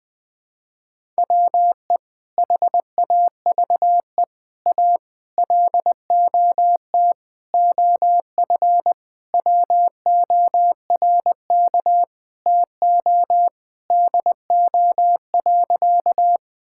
Morse code, heard as a keyed tone, about 20 wpm, 700 Hz. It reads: WE HAVE A LOT OF WORK TO DO.
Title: Sentences with 5-Word Phrases - 20wpm